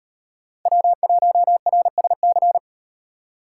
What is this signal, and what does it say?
Morse code, heard as a keyed tone, about 38 wpm, 700 Hz. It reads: W1RSC